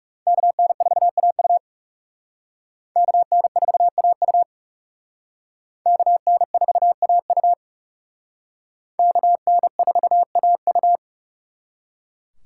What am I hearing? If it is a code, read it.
Morse code, heard as a keyed tone, about 37 wpm, 700 Hz. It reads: KN4AU KN4AU KN4AU KN4AU
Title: Speed Racing - USA 2x2 Call Signs 30wpm